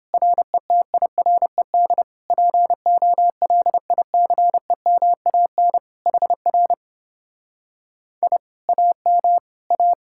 Morse code, heard as a keyed tone, about 29 wpm, 700 Hz. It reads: RETIRED POLICEMAN HR I AM A